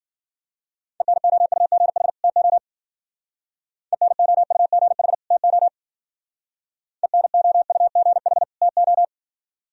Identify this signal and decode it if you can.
Morse code, heard as a keyed tone, about 48 wpm, 700 Hz. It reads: ENOUGH TO ENOUGH TO ENOUGH TO
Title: Speed-Racing - Sets of 2 Words 40wpm